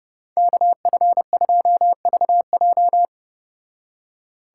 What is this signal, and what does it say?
Morse code, heard as a keyed tone, about 29 wpm, 700 Hz. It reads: KF2VJ